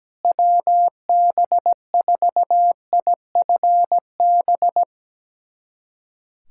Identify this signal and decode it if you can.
Morse code, heard as a keyed tone, about 17 wpm, 700 Hz. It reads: WB4IFB
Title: USA Callsign DE Callsign 17wpm